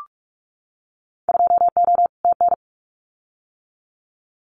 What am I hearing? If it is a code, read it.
Morse code, heard as a keyed tone, about 44 wpm, 700 Hz. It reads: 2O TN
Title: Winter Field Day Exchanges 45wpm